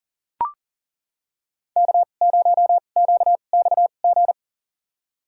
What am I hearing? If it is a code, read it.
Morse code, heard as a keyed tone, about 39 wpm, spaced as for 29 wpm, 700 Hz. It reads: K0QXG
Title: USA Call Signs - 5 Characters - 2x Character Spacing - 40wpm